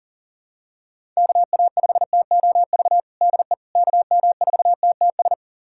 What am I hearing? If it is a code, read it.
Morse code, heard as a keyed tone, about 40 wpm, 700 Hz. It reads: KA5TOV DE KM4TTS
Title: USA Callsign DE Callsign - 6 Characters - 40wpm